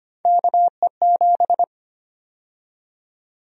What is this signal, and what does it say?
Morse code, heard as a keyed tone, about 25 wpm, 700 Hz. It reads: KE7
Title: USA Call Sign Prefixes - 25wpm